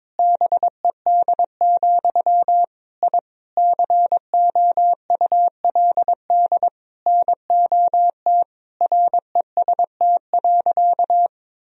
Morse code, heard as a keyed tone, about 22 wpm, 700 Hz. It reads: BED, I COULD NOT REST.
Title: Sentences from Top 1000 Words - Review Part 2 - 22wpm